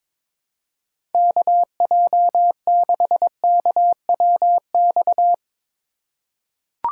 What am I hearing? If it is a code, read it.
Morse code, heard as a keyed tone, about 23 wpm, 700 Hz. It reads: KJ6KWX
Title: USA Call Signs - 6 Characters - 4x Character Spacing - 22wpm